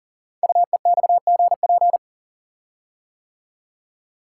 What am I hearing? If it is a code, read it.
Morse code, heard as a keyed tone, about 42 wpm, 690 Hz. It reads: UEXGP
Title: Code Groups - 5 Letters - 40wpm